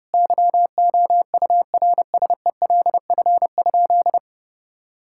Morse code, heard as a keyed tone, about 30 wpm, 700 Hz. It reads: YOURSELF?
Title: Sentences from Top 1600 Words 30wpm